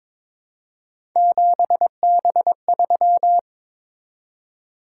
Morse code, heard as a keyed tone, about 22 wpm, 700 Hz. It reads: 7B3
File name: practice / 22wpm / Three Letter-Number Combinations - 22wpm